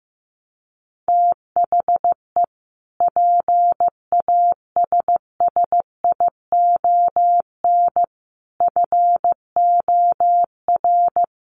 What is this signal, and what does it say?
Morse code, heard as a keyed tone, about 15 wpm, 700 Hz. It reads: THE PASSION FOR
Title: Sentences from Top 1300 Words 15wpm